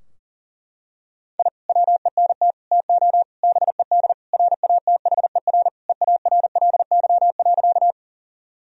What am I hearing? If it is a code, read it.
Morse code, heard as a keyed tone, about 40 wpm, 700 Hz. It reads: I WENT TO BED RATHER EARLY.
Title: Sentences from Top 1000 Words - Review Part 4 - 40wpm